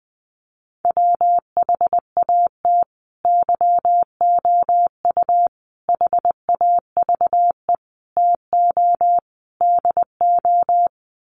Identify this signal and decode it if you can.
Morse code, heard as a keyed tone, about 20 wpm, 700 Hz. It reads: WHAT YOU HAVE TO DO
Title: Sets of 5 Words Encore 20wpm